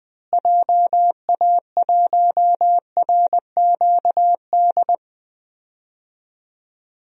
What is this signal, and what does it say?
Morse code, heard as a keyed tone, about 20 wpm, 700 Hz. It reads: JA1RQD
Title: DXCC-100 Call Signs - 6 Characters - Repeated 3x - 20wpm